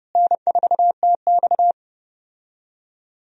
Morse code, heard as a keyed tone, about 30 wpm, 700 Hz. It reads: N4TX
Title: USA Call Signs - 4 Characters - Repeated 3x - 30wpm